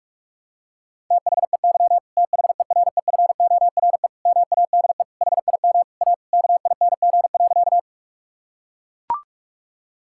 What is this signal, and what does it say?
Morse code, heard as a keyed tone, about 44 wpm, 700 Hz. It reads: THEY THEREFORE MADE HIM A KING.
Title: Sentences from Top 1000 Words - Review Part 5 - 45wpm